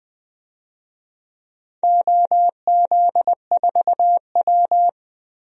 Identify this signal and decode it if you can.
Morse code, heard as a keyed tone, about 20 wpm, 700 Hz. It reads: OZ4W
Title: DXCC-100 Call Signs - 4 Characters - 20wpm